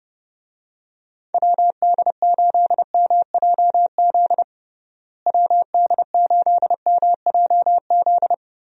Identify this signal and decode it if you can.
Morse code, heard as a keyed tone, about 30 wpm, 700 Hz. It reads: WD8MJZ WD8MJZ
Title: USA Call Signs - 6 Characters - Repeated 3x - 30wpm